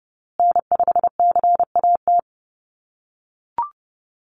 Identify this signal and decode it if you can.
Morse code, heard as a keyed tone, about 30 wpm, 700 Hz. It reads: N5CAT
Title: USA Call Signs Repeated 2x 30wpm